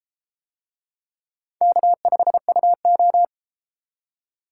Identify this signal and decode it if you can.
Morse code, heard as a keyed tone, about 31 wpm, 700 Hz. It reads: K5UO